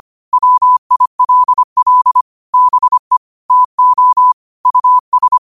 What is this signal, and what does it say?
Morse code, heard as a keyed tone, about 25 wpm, 1000 Hz. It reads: WILL BE TO US